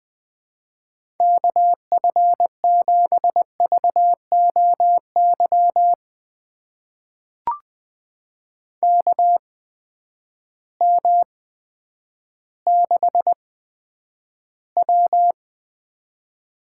Morse code, heard as a keyed tone, about 21 wpm, spaced as for 10 wpm, 700 Hz. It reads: KF7VOY KM6W
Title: USA Call Signs - 6 Characters - 8x Character Spacing - 20wpm